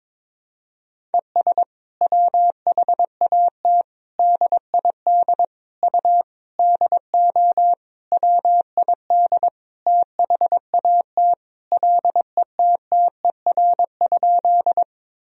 Morse code, heard as a keyed tone, about 22 wpm, 700 Hz. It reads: ES WHAT DID U DO WID THAT LETTER?
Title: Sentences from Top 300 Words and Abbreviations 22wpm